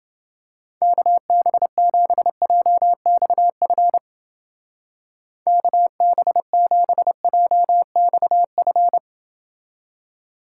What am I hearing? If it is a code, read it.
Morse code, heard as a keyed tone, about 30 wpm, 700 Hz. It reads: KB7JXF KB7JXF